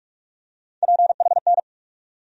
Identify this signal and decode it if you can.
Morse code, heard as a keyed tone, about 47 wpm, 700 Hz. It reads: PHN
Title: USA Call Sign Postfixes - 3 Letters 45wpm